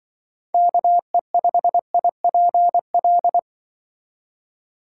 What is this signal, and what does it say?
Morse code, heard as a keyed tone, about 24 wpm, 700 Hz. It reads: KE5IPL
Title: Speed Racing - USA 2x3 Call Signs 20wpm